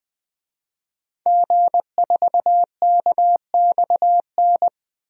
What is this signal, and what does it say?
Morse code, heard as a keyed tone, about 20 wpm, 700 Hz. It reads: G4KXN
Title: DXCC-100 Call Signs - 5 Characters - Repeated 2x - 20wpm